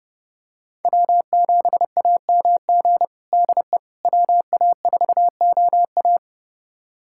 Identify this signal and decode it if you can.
Morse code, heard as a keyed tone, about 30 wpm, 700 Hz. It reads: W7AMG DE WA4OA
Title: USA Callsign DE Callsign - 5 Characters - 30wpm